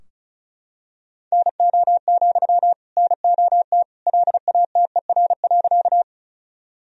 Morse code, heard as a keyed tone, about 36 wpm, 700 Hz. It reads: NO, NOT LATER.
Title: Sentences from Top 1000 Words - Review Part 4 - 35wpm